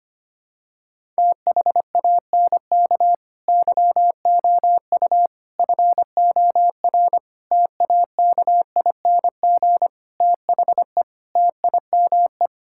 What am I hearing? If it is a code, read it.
Morse code, heard as a keyed tone, about 25 wpm, 700 Hz. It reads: THANK YOU FOR TAKING THE TIME
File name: Sets of 9 Words - 25wpm